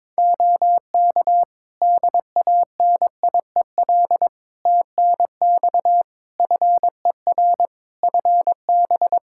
Morse code, heard as a keyed tone, about 21 wpm, 700 Hz. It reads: OK DANIEL TNX FER FB
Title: K1USN SST QSO Practice 22wpm